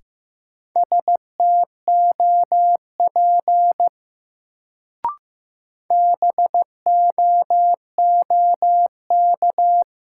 Morse code, heard as a keyed tone, about 15 wpm, 700 Hz. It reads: STOP BOOK
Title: Sentences from Top 300 Words - New Word Drill - 15wpm